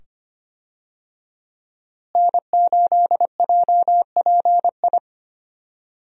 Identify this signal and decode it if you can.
Morse code, heard as a keyed tone, about 25 wpm, 700 Hz. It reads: N8JPI
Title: USA Call Signs - 5 Characters - Repeated 2x - 25wpm